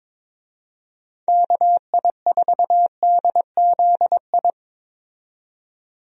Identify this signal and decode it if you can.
Morse code, heard as a keyed tone, about 22 wpm, 700 Hz. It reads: KI4DZI
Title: USA Call Signs - 6 Characters - 22wpm